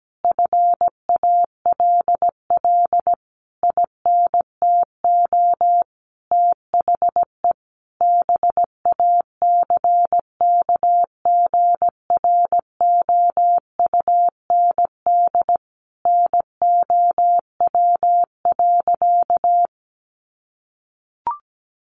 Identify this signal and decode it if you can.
Morse code, heard as a keyed tone, about 17 wpm, 700 Hz. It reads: FALL INTO THE BACKGROUND NOW.
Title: Sentences from Top 1300 Words 17wpm